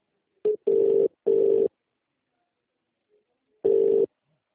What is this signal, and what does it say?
Ringback tone (UK), 400 + 450 Hz.